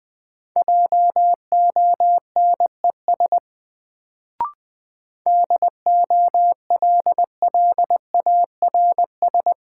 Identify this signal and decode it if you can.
Morse code, heard as a keyed tone, about 20 wpm, 700 Hz. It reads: JONES DOLLARS
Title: Sets of 5 Words - New Word Drill - 20wpm